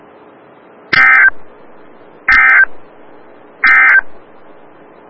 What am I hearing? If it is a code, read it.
Emergency Alert System (SAME) tones, heard as harsh buzzing bursts. It is heard from 0.8 to 4.1 s.
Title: Des Moines, IA NOAA Weather Radio - WXL57